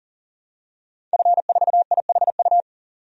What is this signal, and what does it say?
Morse code, heard as a keyed tone, about 40 wpm, 700 Hz. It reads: F4IHU
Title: DXCC-100 Call Signs - 5 Characters - 40wpm